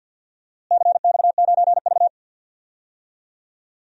Morse code, heard as a keyed tone, about 54 wpm, 700 Hz. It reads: CX9V
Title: DXCC-100 Call Signs - 4 Characters - Repeated 2x - 50wpm